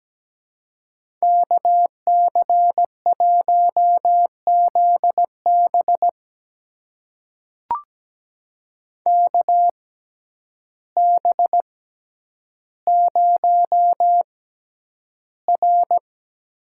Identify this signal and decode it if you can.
Morse code, heard as a keyed tone, about 17 wpm, spaced as for 9 wpm, 700 Hz. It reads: KC1ZB KB0R
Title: USA Call Signs - 5 Characters - 6x Character Spacing - 17wpm